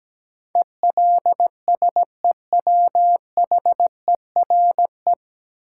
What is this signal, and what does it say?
Morse code, heard as a keyed tone, about 17 wpm, 700 Hz. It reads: ELSEWHERE
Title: Sentences from Top 1600 Words - New Word Drill Alt - 17wpm